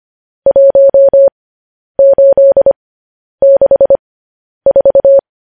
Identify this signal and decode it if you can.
Morse code, heard as a keyed tone, about 25 wpm, spaced as for 10 wpm, 550 Hz. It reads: 1864